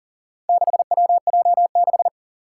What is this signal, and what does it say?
Morse code, heard as a keyed tone, about 42 wpm, 700 Hz. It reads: 6WJ6
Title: Four Challenging Letter-Number Combinations 40wpm